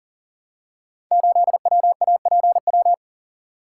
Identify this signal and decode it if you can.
Morse code, heard as a keyed tone, about 39 wpm, 690 Hz. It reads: 8WAPW
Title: Code Groups - 5 Letter:Numbers - 40wpm